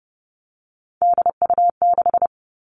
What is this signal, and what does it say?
Morse code, heard as a keyed tone, about 30 wpm, 700 Hz. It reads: DU6
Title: All Cut Numbers - 3-Digit 30wpm